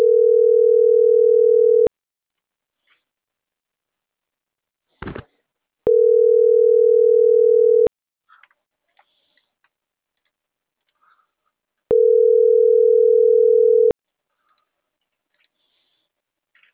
Ringback tone (North America), 440 + 480 Hz.